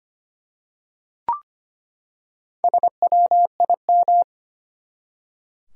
Morse code, heard as a keyed tone, about 25 wpm, 700 Hz. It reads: SWIM
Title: Four Letter Words - 25wpm